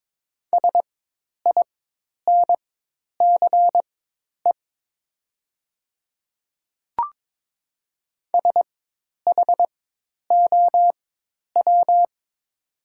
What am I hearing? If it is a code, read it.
Morse code, heard as a keyed tone, about 23 wpm, spaced as for 10 wpm, 700 Hz. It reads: SINCE SHOW
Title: 4x Character Spacing - Top 300 Words - 22wpm